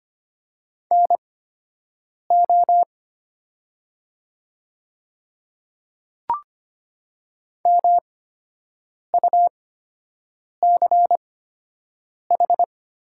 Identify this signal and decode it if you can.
Morse code, heard as a keyed tone, about 25 wpm, spaced as for 7 wpm, 700 Hz. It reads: NO MUCH
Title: 8x Character Spacing - Top 200 Words - 25wpm